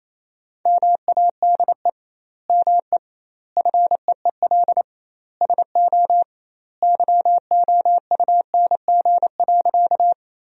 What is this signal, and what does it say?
Morse code, heard as a keyed tone, about 28 wpm, 700 Hz. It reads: MADE ME FEEL SO YOUNG.
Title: 2x Word-Spacing - Sentences from Top 400 Words - 28wpm